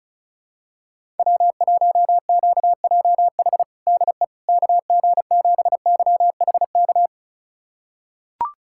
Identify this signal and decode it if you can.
Morse code, heard as a keyed tone, about 36 wpm, 700 Hz. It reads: W1QJH DE KG7YHK
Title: USA Callsign DE Callsign 35wpm